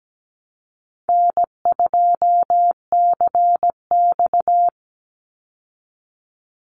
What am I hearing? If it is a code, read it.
Morse code, heard as a keyed tone, about 17 wpm, 700 Hz. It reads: N2CX
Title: Call Signs Repeated 3x - 17wpm